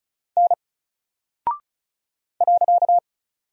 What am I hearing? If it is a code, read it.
Morse code, heard as a keyed tone, about 36 wpm, 700 Hz. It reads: N .
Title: Single Character (Letter:Number:Punctuation:Prosign) Alt 35wpm